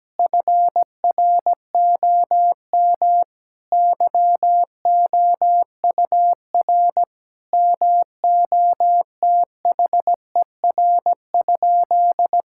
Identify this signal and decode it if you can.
Morse code, heard as a keyed tone, about 17 wpm, 700 Hz. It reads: FROM YOUR MOTHER?
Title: Sentences from Top 1600 Words 17wpm